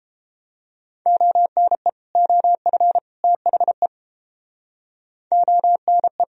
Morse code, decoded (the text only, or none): ONE OF THE ONE